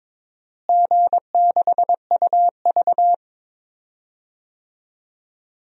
Morse code, heard as a keyed tone, about 22 wpm, 700 Hz. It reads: G6UV
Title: DXCC-100 Call Signs - 4 Characters - Repeated 3x - 22wpm